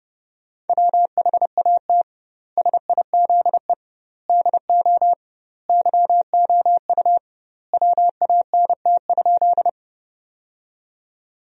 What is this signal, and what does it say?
Morse code, heard as a keyed tone, about 30 wpm, 700 Hz. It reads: WHAT SIZE DO YOU WANT?
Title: 2x Word-Spacing - Sentences from Top 400 Words - 30wpm